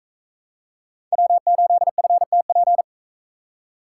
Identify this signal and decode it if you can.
Morse code, heard as a keyed tone, about 39 wpm, 700 Hz. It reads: W8FTP